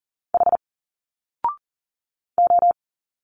Morse code, heard as a keyed tone, about 40 wpm, 700 Hz. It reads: H O
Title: Single Letters 40wpm